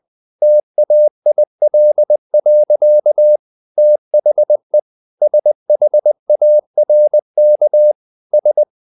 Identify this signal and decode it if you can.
Morse code, heard as a keyed tone, about 20 wpm, 600 Hz. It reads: TAIL. THE SHARK S
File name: the old man and the sea part 2 20wpm